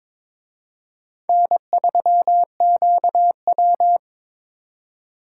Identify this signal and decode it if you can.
Morse code, heard as a keyed tone, about 26 wpm, 700 Hz. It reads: N3QW
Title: Speed Racing - USA 1x2 Call Signs 22wpm